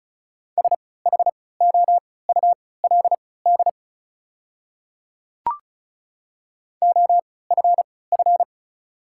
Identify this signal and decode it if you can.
Morse code, heard as a keyed tone, about 36 wpm, spaced as for 20 wpm, 690 Hz. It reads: SHOULD OFF
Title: 3x Character Spacing - Top 500 Words - 35wpm